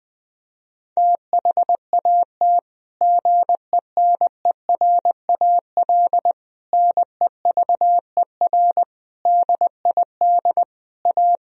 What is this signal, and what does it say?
Morse code, heard as a keyed tone, about 20 wpm, 700 Hz. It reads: THAT GENERAL NEVER DID A
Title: Sentences from Top 1000 Words - Review Part 4 - 20wpm